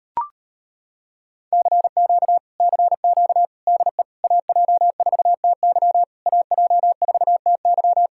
Morse code, heard as a keyed tone, about 33 wpm, 700 Hz. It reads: CQ CQ DE AJ4TY AJ4TY